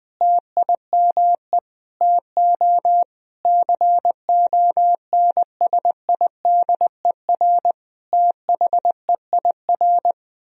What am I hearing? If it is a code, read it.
Morse code, heard as a keyed tone, about 20 wpm, 700 Hz. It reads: TIME TO CONSIDER THEIR
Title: Sentences from Top 1000 Words - Review Part 4 - 20wpm